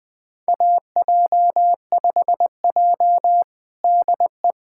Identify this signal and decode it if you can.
Morse code, heard as a keyed tone, about 20 wpm, 700 Hz. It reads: AJ5J DE
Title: USA Callsign DE Callsign - 4 Characters - 20wpm